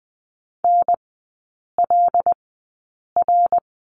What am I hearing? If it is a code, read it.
Morse code, heard as a keyed tone, about 20 wpm, spaced as for 8 wpm, 700 Hz. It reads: NLR
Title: 133 Lesson 22e - Question Mark - Three Characters 20wpm